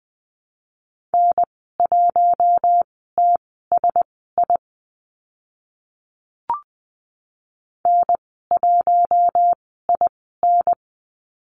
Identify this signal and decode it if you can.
Morse code, heard as a keyed tone, about 20 wpm, spaced as for 15 wpm, 700 Hz. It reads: N1TSI N1IN
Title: 042 Lesson 8g - Number 1 - Call Signs 20wpm